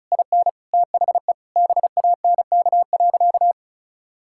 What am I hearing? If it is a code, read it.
Morse code, heard as a keyed tone, about 36 wpm, 700 Hz. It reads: IN THE BANK.
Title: Sentences from Top 1000 Words - Review Part 6 - 35wpm